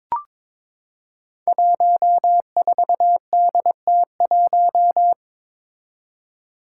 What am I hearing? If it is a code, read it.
Morse code, heard as a keyed tone, about 22 wpm, 700 Hz. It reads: 14DT1
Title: Five Challenging Letter-Number Combinations 22wpm